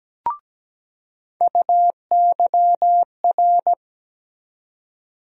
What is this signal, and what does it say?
Morse code, heard as a keyed tone, about 17 wpm, 700 Hz. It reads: UYR
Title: USA Call Sign Postfixes - 3 Letters 17wpm